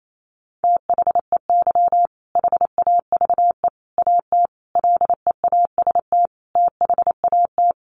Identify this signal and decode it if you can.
Morse code, heard as a keyed tone, about 28 wpm, 700 Hz. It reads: THEY HAVE AT LEAST THAT
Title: Sentences from Top 900 Words 28wpm